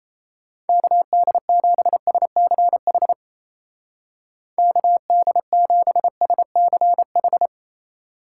Morse code, heard as a keyed tone, about 34 wpm, 700 Hz. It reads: KD7SCH KD7SCH
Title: Speed Racing - USA 2x3 Call Signs 28wpm